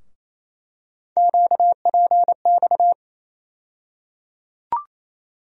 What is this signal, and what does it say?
Morse code, heard as a keyed tone, about 28 wpm, 700 Hz. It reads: QPX
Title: USA Call Sign Postfixes - 3 Letters 28wpm